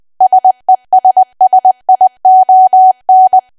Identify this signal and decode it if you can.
Morse code, heard as a keyed tone, about 20 wpm, 750 Hz. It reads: SESSION
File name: cwa / W1AW-Level3 / 20WPM